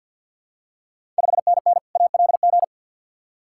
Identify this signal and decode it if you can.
Morse code, heard as a keyed tone, about 49 wpm, 700 Hz. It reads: 5NN ALG
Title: Ontario QSO Party Exchange 50wpm